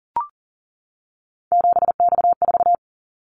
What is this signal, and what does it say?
Morse code, heard as a keyed tone, about 38 wpm, 700 Hz. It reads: 7X4
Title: Callsign Prefixes for Top 100 DXCC - 40wpm